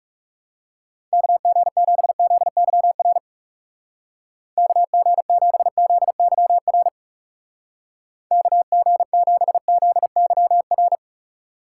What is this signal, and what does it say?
Morse code, heard as a keyed tone, about 39 wpm, 700 Hz. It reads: KG7ZYR KG7ZYR KG7ZYR